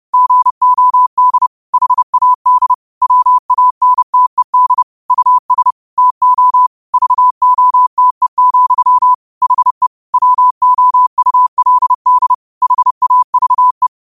Morse code, heard as a keyed tone, about 30 wpm, 1000 Hz. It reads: GOD HAD WANTED US TO VOTE, HE WOULD HAVE
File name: qotdfiles / 30wpm